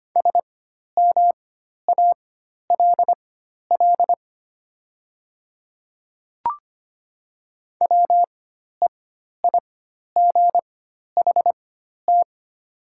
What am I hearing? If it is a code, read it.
Morse code, heard as a keyed tone, about 25 wpm, spaced as for 12 wpm, 700 Hz. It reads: SMALL WEIGHT
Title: 4x Character Spacing - Top 500 Words - 25wpm